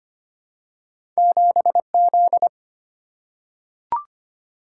Morse code, heard as a keyed tone, about 25 wpm, 700 Hz. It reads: 7Z
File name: Two Letter-Number Combinations - 25wpm